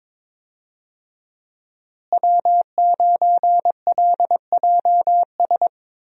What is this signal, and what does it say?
Morse code, heard as a keyed tone, about 22 wpm, 700 Hz. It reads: W9LJS
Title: USA Call Signs - 5 Characters - Repeated 3x - 22wpm